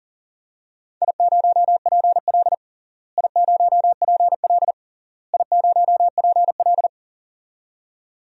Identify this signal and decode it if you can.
Morse code, heard as a keyed tone, about 40 wpm, 700 Hz. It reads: I0PL I0PL I0PL
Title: DXCC-100 Call Signs - 4 Characters - Repeated 3x - 40wpm